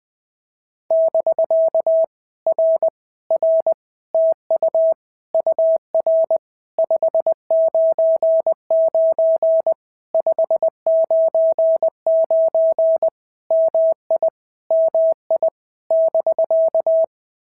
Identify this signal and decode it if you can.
Morse code, heard as a keyed tone, about 20 wpm, 650 Hz. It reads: <BK> R R TU UR 599 599 MI MI <BK>
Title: POTA QSO Practice - 20wpm